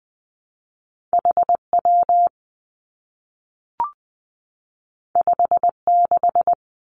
Morse code, heard as a keyed tone, about 20 wpm, 700 Hz. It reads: H W 5 6
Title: Two Challenging Letter-Number Combinations 20wpm